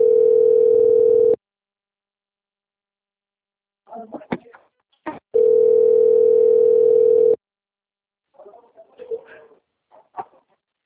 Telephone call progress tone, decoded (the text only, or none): ringback tone